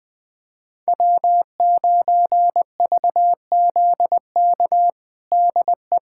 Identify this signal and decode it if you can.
Morse code, heard as a keyed tone, about 20 wpm, 700 Hz. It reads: W9VZK DE
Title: USA Callsign DE Callsign - 5 Characters - 20wpm